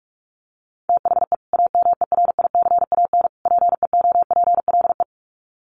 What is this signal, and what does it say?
Morse code, heard as a keyed tone, about 44 wpm, 700 Hz. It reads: THE AMERICAN PEOPLE
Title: Sets of 3 Words 45wpm